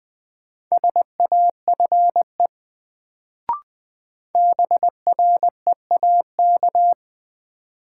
Morse code, heard as a keyed tone, about 20 wpm, 700 Hz. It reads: SAFE BREAK
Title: Sentences from Top 900 Words - New Word Drill - 20wpm